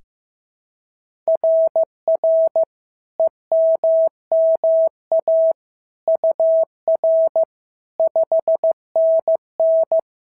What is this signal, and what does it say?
Morse code, heard as a keyed tone, about 14 wpm, 650 Hz. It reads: RR EMMA UR 5NN
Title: SOTA QSO Practice 15wpm